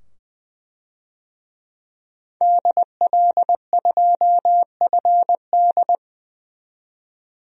Morse code, heard as a keyed tone, about 20 wpm, 700 Hz. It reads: DL2FD
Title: DXCC-100 Call Signs - 5 Characters - 20wpm